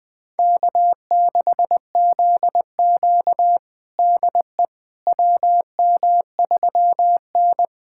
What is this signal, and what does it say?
Morse code, heard as a keyed tone, about 21 wpm, 700 Hz. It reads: K6ZQ DE WM3N
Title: USA Callsign DE Callsign - 4 Characters - 20wpm